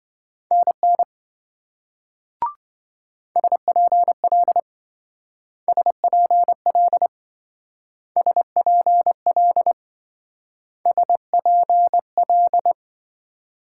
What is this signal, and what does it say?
Morse code, heard as a keyed tone, about 25 wpm, 700 Hz. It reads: NN SPL SPL SPL SPL